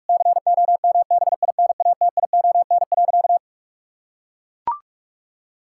Morse code, heard as a keyed tone, about 41 wpm, 700 Hz. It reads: COMBINATION.
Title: Sentences from Top 1000 Words - Review Part 6 - 45wpm